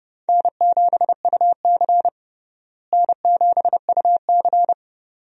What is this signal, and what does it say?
Morse code, heard as a keyed tone, about 30 wpm, 700 Hz. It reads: N7UC N7UC
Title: USA Call Signs - 4 Characters - Repeated 2x - 30wpm